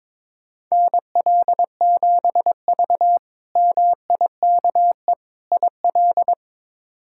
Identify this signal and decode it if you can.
Morse code, heard as a keyed tone, about 22 wpm, 700 Hz. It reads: NL7V MIKE IL
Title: K1USN SST QSO Practice - 22wpm